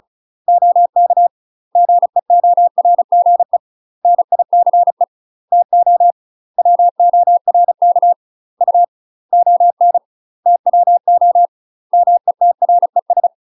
Morse code, heard as a keyed tone, about 36 wpm, 700 Hz. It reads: OK GEORGE NICE TO WORK U ON TWO METRES
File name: UK MASTER QSOs TRAINING / 35WPM / UK MASTER QSOs 0016